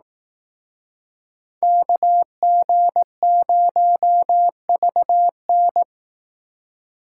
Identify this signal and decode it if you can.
Morse code, heard as a keyed tone, about 18 wpm, 700 Hz. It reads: KG0VN
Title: Speed Racing - USA 2x2 Call Signs 15wpm